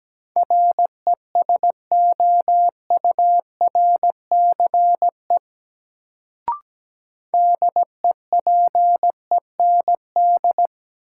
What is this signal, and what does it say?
Morse code, heard as a keyed tone, about 17 wpm, 700 Hz. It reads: RESOURCE DEPEND
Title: Sentences from Top 1300 Words - New Word Drill - 17wpm